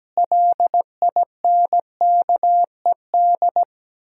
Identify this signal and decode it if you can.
Morse code, heard as a keyed tone, about 17 wpm, 700 Hz. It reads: LINKED
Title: Pangram Challenge v2 - 17wpm